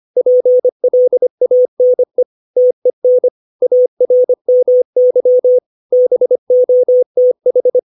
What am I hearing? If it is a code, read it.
Morse code, heard as a keyed tone, about 25 wpm, 500 Hz. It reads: PLANE TEN ARMY BOTH
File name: 632 words / 632 Words 25WPM